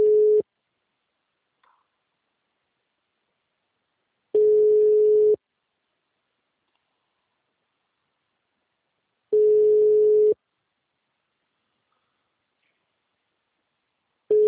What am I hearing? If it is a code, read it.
Ringback tone (Europe, 425 Hz).